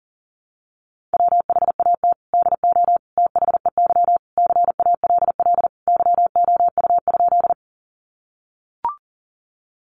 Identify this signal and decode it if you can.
Morse code, heard as a keyed tone, about 40 wpm, 700 Hz. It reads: WHAT DO THEY CALL YOU?
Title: Sentences from Top 100 Words 40wpm